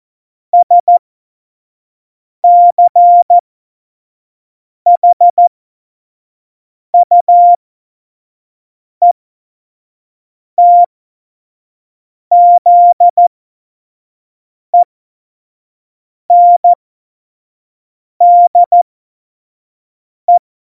Morse code, heard as a keyed tone, about 14 wpm, spaced as for 5 wpm, 700 Hz. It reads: SCHUETZENDE
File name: Morsekurs / Witze / Witz 08. Dezember2020, 25 F70